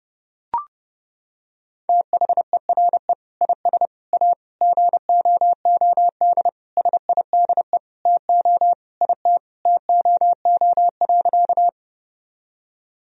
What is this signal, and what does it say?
Morse code, heard as a keyed tone, about 31 wpm, 700 Hz. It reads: THERE IS A GOOD SIDE TO IT TOO.
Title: Sentences from Top 1000 Words - Review Part 1 - 30wpm